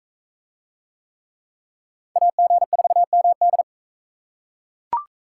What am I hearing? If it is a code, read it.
Morse code, heard as a keyed tone, about 42 wpm, 700 Hz. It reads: AG4MD